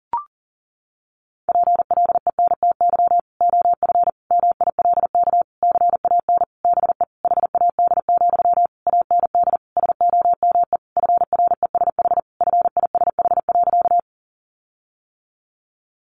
Morse code, heard as a keyed tone, about 40 wpm, 700 Hz. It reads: PLENTY OF MILK CAN BE HAD, AND SOME FRESH FISH.
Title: Sentences from Top 1500 Words 40wpm